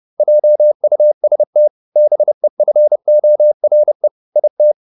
Morse code, heard as a keyed tone, about 30 wpm, 600 Hz. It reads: JUST BEFORE IT
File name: the old man and the sea 2 30wpm